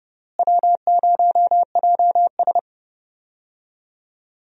Morse code, heard as a keyed tone, about 30 wpm, 700 Hz. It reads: W0JS
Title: USA Call Signs - 4 Characters - Repeated 3x - 30wpm